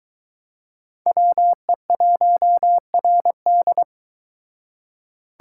Morse code, heard as a keyed tone, about 22 wpm, 700 Hz. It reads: WE1RD